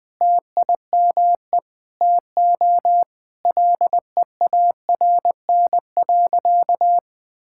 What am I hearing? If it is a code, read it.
Morse code, heard as a keyed tone, about 21 wpm, 700 Hz. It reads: TIME TO LEARN.
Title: Sentences with 8-Word Phrases - 20wpm